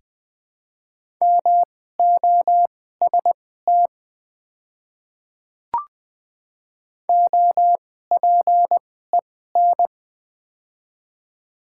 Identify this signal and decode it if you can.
Morse code, heard as a keyed tone, about 20 wpm, spaced as for 14 wpm, 700 Hz. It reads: MOST OPEN
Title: 2x Character Spacing - Top 300 Words - 20wpm